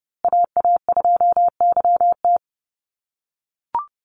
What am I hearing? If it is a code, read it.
Morse code, heard as a keyed tone, about 30 wpm, 700 Hz. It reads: AA2YT
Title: USA Call Signs Repeated 3x 30wpm